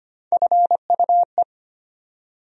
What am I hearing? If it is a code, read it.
Morse code, heard as a keyed tone, about 25 wpm, 700 Hz. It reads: FUE
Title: USA Call Sign Postfixes - 3 Letters 25wpm